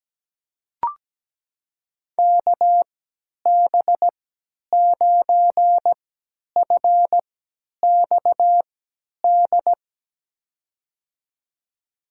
Morse code, heard as a keyed tone, about 17 wpm, spaced as for 10 wpm, 700 Hz. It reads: KB9FXD
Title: USA Call Signs - 3x Character Spacing - 17wpm